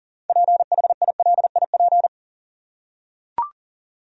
Morse code, heard as a keyed tone, about 40 wpm, 700 Hz. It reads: PHILIP
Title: USA Common Names - 40wpm